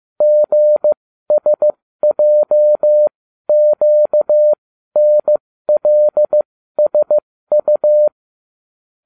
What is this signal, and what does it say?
Morse code, heard as a keyed tone, about 15 wpm, spaced as for 12 wpm, 600 Hz. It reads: GSJQNLSU